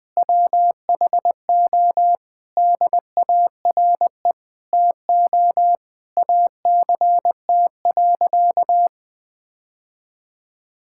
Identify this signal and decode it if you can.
Morse code, heard as a keyed tone, about 20 wpm, 700 Hz. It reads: WHO DARE TO ACT.
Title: Difficult Sentences with Hints - 20wpm